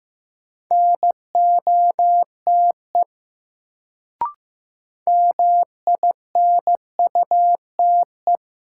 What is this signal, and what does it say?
Morse code, heard as a keyed tone, about 15 wpm, 700 Hz. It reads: NOTE MINUTE
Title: Sentences from Top 500 Words - New Word Drill - 15wpm